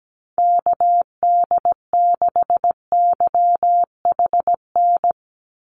Morse code, heard as a keyed tone, about 17 wpm, 700 Hz. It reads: KD6YHN
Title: USA Call Signs 17wpm